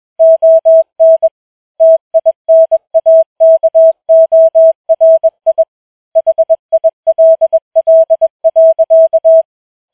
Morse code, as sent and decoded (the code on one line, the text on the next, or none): --- -. / - .. -. .- -.- --- .-. .. / .... .. .-.. .-.. .-.-.-
ON TINAKORI HILL.